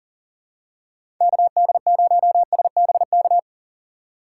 Morse code, heard as a keyed tone, about 40 wpm, 700 Hz. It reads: KD0SBK
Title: USA Call Signs - 4x Character Spacing - 40wpm